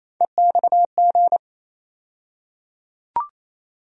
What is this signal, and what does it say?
Morse code, heard as a keyed tone, about 27 wpm, 700 Hz. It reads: EXG
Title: USA Call Sign Postfixes - 3 Letters 28wpm